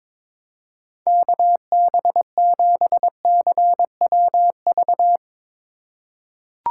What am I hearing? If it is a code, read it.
Morse code, heard as a keyed tone, about 22 wpm, 700 Hz. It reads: KB7CWV
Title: USA Call Signs - 3x Character Spacing - 22wpm